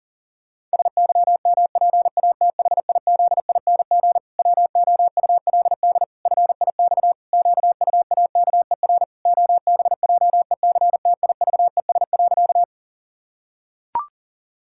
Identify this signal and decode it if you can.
Morse code, heard as a keyed tone, about 40 wpm, 700 Hz. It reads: SYMPATHIZING WOULD FIX QUAKER OBJECTIVES.
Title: Pangram Challenge v2 - 40wpm